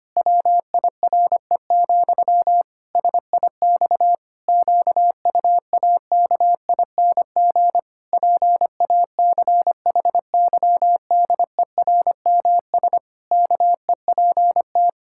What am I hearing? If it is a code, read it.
Morse code, heard as a keyed tone, about 25 wpm, 700 Hz. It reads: WIRE, SIX QUAKING PACHYDERMS KEPT
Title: Pangram Challenge v2 - 25wpm